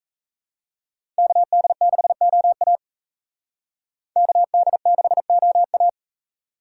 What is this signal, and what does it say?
Morse code, heard as a keyed tone, about 36 wpm, 700 Hz. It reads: KD6OA KD6OA